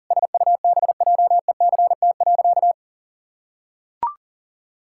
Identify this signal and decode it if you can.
Morse code, heard as a keyed tone, about 37 wpm, 700 Hz. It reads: SUBJECT.
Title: Sentences from Top 1600 Words 40wpm